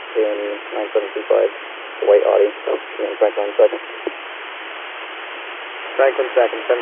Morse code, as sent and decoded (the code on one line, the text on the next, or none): none